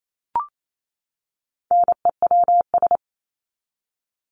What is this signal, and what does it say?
Morse code, heard as a keyed tone, about 28 wpm, 700 Hz. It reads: NEWS
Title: Top 751-1000 Words 28wpm